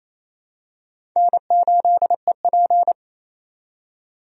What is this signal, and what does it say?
Morse code, heard as a keyed tone, about 28 wpm, 700 Hz. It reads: N8EP
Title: USA Call Signs - 4 Characters - 28wpm